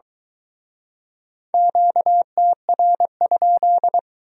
Morse code, heard as a keyed tone, about 22 wpm, 700 Hz. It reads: QTR?